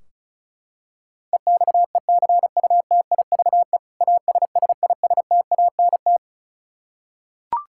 Morse code, heard as a keyed tone, about 36 wpm, 700 Hz. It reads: EXECUTIVE ASSISTANT
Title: Practice QSO - Exchange 3 - More Info - 35wpm